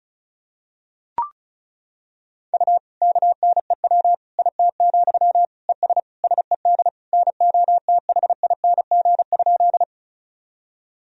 Morse code, heard as a keyed tone, about 34 wpm, 700 Hz. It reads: U KNEW IT, ES SED NOTHING?
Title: Sentences from Top 500 Words and Abbreviations 35wpm